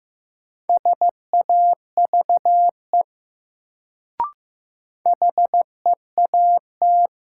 Morse code, heard as a keyed tone, about 15 wpm, 700 Hz. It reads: SAVE HEAT
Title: Sentences from Top 1100 Words - New Word Drill Alt - 15wpm